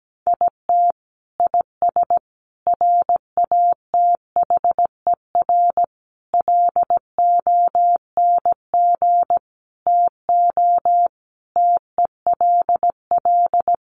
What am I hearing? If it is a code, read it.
Morse code, heard as a keyed tone, about 17 wpm, 700 Hz. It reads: IT IS RATHER LONG TO TELL
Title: Sentences from Top 700 Words 17wpm